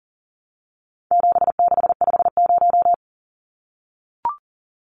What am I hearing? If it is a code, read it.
Morse code, heard as a keyed tone, about 38 wpm, 700 Hz. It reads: 7650
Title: Numbers - 4-Digit 40wpm